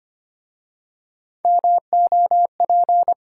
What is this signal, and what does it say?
Morse code, heard as a keyed tone, about 25 wpm, 690 Hz. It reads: MOP
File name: practice / 25wpm / Easy Words with Hints - 25wpm